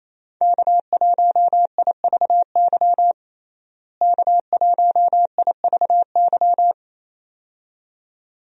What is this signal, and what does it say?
Morse code, heard as a keyed tone, about 28 wpm, 700 Hz. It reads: K1IVY K1IVY
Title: USA Call Signs - 5 Characters - Repeated 2x - 28wpm